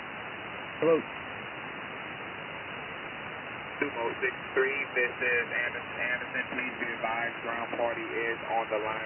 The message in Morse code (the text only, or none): none